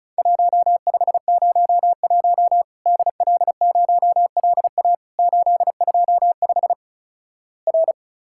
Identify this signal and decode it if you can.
Morse code, heard as a keyed tone, about 36 wpm, 700 Hz. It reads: 1501 DL0LA 825 R
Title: Worked All Europe DX Contest - QTC Exchange - 35wpm